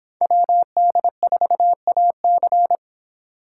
Morse code, heard as a keyed tone, about 29 wpm, 700 Hz. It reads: WD4AC